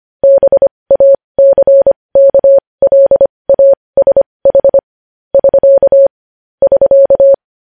Morse code, heard as a keyed tone, about 25 wpm, spaced as for 20 wpm, 550 Hz. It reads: BACKLASH <SK> <SK>
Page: news-headlines-in-morse-code-at-20-wpm